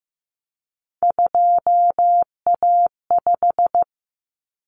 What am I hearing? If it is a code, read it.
Morse code, heard as a keyed tone, about 15 wpm, 700 Hz. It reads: 2A5
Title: Common Cut Numbers - 3-Digit 15wpm